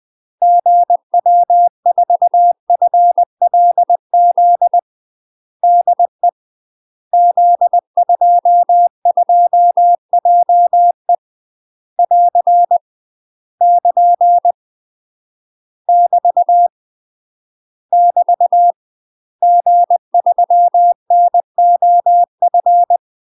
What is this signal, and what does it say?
Morse code, heard as a keyed tone, about 20 wpm, 700 Hz. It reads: GW4FLZ DE Z22JE <AR> <KN> <BT> <BT> G3NOF
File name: UK MASTER QSOs TRAINING / 20WPM / UK MASTER QSOs 0035